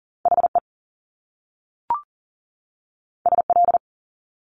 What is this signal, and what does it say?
Morse code, heard as a keyed tone, about 39 wpm, 700 Hz. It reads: HE SL
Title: Top Two Letter Combinations in Words 40wpm